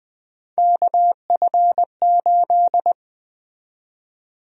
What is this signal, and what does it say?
Morse code, heard as a keyed tone, about 20 wpm, 700 Hz. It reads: KF8
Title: USA Call Sign Prefixes - 20wpm